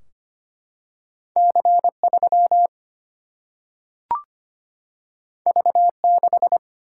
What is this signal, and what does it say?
Morse code, heard as a keyed tone, about 25 wpm, 700 Hz. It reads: C3 V6
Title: Single-Letter Single-Number 25wpm